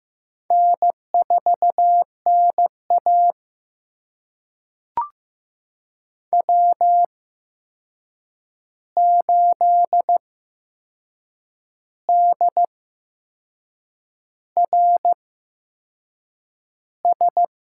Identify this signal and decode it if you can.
Morse code, heard as a keyed tone, about 15 wpm, spaced as for 6 wpm, 700 Hz. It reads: N4NA W8DRS
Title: USA Call Signs - 8x Character Spacing - 15wpm